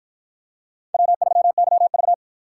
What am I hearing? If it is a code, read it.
Morse code, heard as a keyed tone, about 53 wpm, 700 Hz. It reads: W3YV